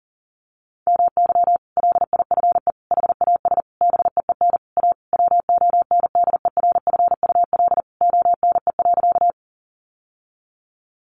Morse code, heard as a keyed tone, about 40 wpm, 700 Hz. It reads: MY LIFE HAS BEEN A WONDERFUL ONE.
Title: Sentences from Top 1100 Words 40wpm